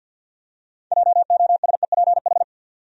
Morse code, heard as a keyed tone, about 47 wpm, 700 Hz. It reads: JOSEPH
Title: USA Common Names - 50wpm